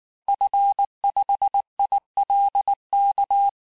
Morse code, heard as a keyed tone, about 19 wpm, 800 Hz. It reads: F5ILK
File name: cours-du-lundi files / cours-1595--FIN-de-QSO-vitesse-18mpm